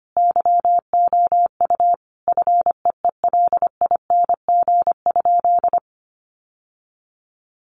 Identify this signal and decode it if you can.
Morse code, heard as a keyed tone, about 25 wpm, 700 Hz. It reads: YOU FEELING?
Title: Sentences from Top 1200 Words 25wpm